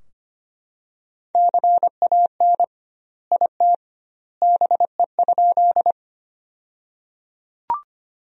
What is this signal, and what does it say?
Morse code, heard as a keyed tone, about 25 wpm, 700 Hz. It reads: CAN IT BE?
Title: 2x Word-Spacing - Sentences from Top 100 Words - 25wpm